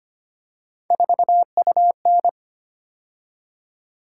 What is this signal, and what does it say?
Morse code, heard as a keyed tone, about 25 wpm, 700 Hz. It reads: 4UN
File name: Three Letter-Number Combinations - 25wpm